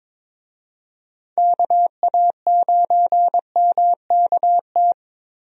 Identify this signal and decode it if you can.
Morse code, heard as a keyed tone, about 22 wpm, 700 Hz. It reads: KA9MKT
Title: USA Call Signs - 6 Characters - Repeated 2x - 22wpm